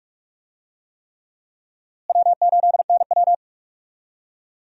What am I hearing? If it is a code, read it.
Morse code, heard as a keyed tone, about 46 wpm, 700 Hz. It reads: W8NW
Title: USA Call Signs - 4 Characters - Repeated 3x - 45wpm